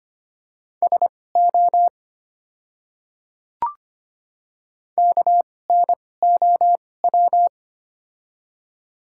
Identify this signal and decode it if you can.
Morse code, heard as a keyed tone, about 25 wpm, spaced as for 18 wpm, 690 Hz. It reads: SO KNOW
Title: 2x Character Spacing - Top 100 Words - 25wpm